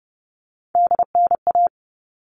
Morse code, heard as a keyed tone, about 30 wpm, 700 Hz. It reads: DNA
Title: All Cut Numbers - 3-Digit 30wpm